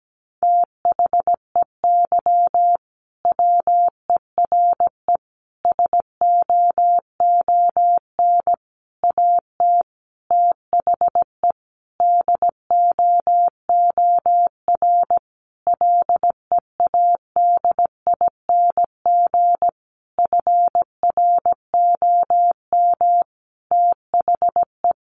Morse code, heard as a keyed tone, about 17 wpm, 700 Hz. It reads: THEY WERE SOON AT THE DOOR LEADING FROM THE
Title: Sentences from Top 1200 Words 17wpm